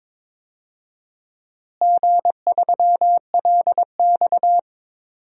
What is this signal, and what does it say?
Morse code, heard as a keyed tone, about 22 wpm, 700 Hz. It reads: G3LX
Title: DXCC-100 Call Signs - 4 Characters - Repeated 2x - 22wpm